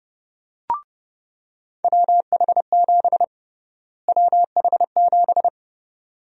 Morse code, heard as a keyed tone, about 29 wpm, 700 Hz. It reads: WH7 WH7
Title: USA Call Sign Prefixes - Repeated 2x - 30wpm